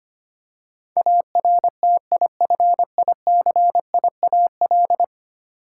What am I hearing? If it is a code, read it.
Morse code, heard as a keyed tone, about 25 wpm, 700 Hz. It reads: ARTIFICIAL
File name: Ten Letter Words - 25wpm